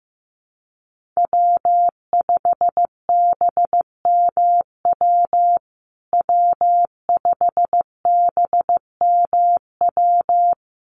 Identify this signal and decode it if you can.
Morse code, heard as a keyed tone, about 15 wpm, 700 Hz. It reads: W5BMW W5BMW
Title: USA Call Signs Repeated 2x 15wpm